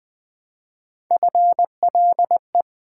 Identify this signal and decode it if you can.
Morse code, heard as a keyed tone, about 20 wpm, 700 Hz. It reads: FLE
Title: Build A Word - 1 Letter At A Time - 4-Letter Words - 20wpm